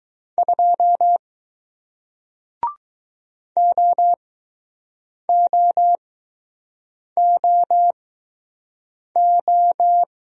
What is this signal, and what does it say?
Morse code, heard as a keyed tone, about 20 wpm, spaced as for 6 wpm, 700 Hz. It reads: 2 OOOO